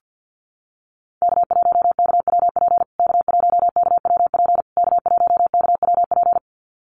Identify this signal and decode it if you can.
Morse code, heard as a keyed tone, about 50 wpm, 700 Hz. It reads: K1KWP K1KWP K1KWP
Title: USA Call Signs Repeated 3x 50wpm